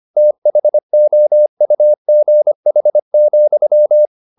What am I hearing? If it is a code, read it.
Morse code, heard as a keyed tone, about 25 wpm, 600 Hz. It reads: THOUGH,
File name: the old man and the sea part 1 25wpm